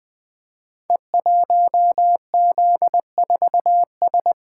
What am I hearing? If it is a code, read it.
Morse code, heard as a keyed tone, about 20 wpm, 700 Hz. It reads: E1Z4S
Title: Code Groups - 5 Letter:Numbers - 20wpm